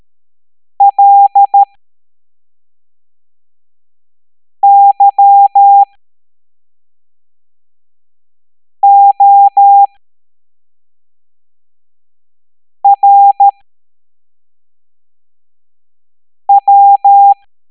Morse code, heard as a keyed tone, about 13 wpm, spaced as for 3 wpm, 800 Hz. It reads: LYORW